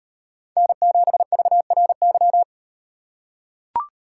Morse code, heard as a keyed tone, about 38 wpm, 700 Hz. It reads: N7VRY